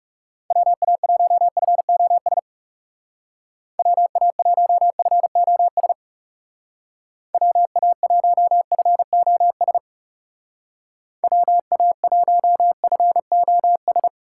Morse code, heard as a keyed tone, about 35 wpm, 700 Hz. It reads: WA1FOS WA1FOS WA1FOS WA1FOS
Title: Speed Racing - USA 2x3 Call Signs 30wpm